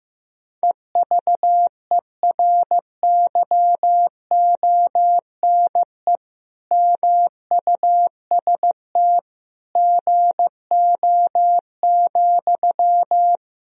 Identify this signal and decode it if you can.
Morse code, heard as a keyed tone, about 15 wpm, 700 Hz. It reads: EVERYONE MUST GO,
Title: Sentences from Top 1000 Words - Review Part 4 - 15wpm